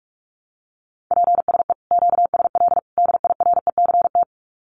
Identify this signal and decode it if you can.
Morse code, heard as a keyed tone, about 44 wpm, 700 Hz. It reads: PSE QSL DIRECT
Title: Common QSO Phrases 45wpm